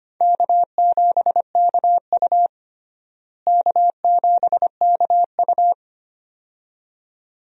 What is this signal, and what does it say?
Morse code, heard as a keyed tone, about 25 wpm, 700 Hz. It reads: K7KU K7KU
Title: USA Call Signs - 4 Characters - Repeated 2x - 25wpm